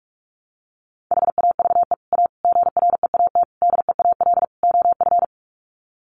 Morse code, heard as a keyed tone, about 44 wpm, 700 Hz. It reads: HAVE A GREAT DEAL OF
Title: Sets of 5 Words Encore2 45wpm